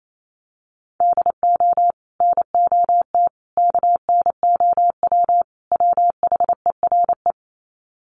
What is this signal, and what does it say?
Morse code, heard as a keyed tone, about 28 wpm, 700 Hz. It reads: DO NOT KNOW WHERE
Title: Sets of 4 Words 28wpm